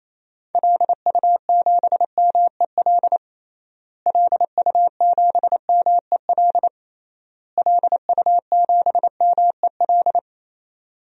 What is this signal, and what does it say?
Morse code, heard as a keyed tone, about 28 wpm, 700 Hz. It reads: LU7MEL LU7MEL LU7MEL
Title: DXCC-100 Call Signs - 6 Characters - Repeated 3x - 28wpm